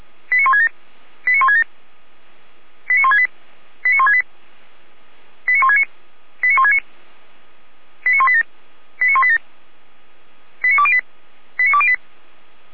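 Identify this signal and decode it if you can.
Five-tone selective call (ZVEI), heard as quick runs of five tones, tones of about 80 ms. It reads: 89167 89167 89168 89168 89169 89169 89187 89187 89208 89208
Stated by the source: ZVEI five-tone selective call received with an SDR, logged by feuerwehr-orlamuende.org fire brigade.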